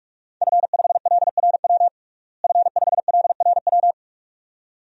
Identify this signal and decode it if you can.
Morse code, heard as a keyed tone, about 46 wpm, 700 Hz. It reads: F5LRW F5LRW
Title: DXCC-100 Call Signs - 5 Characters - Repeated 2x - 45wpm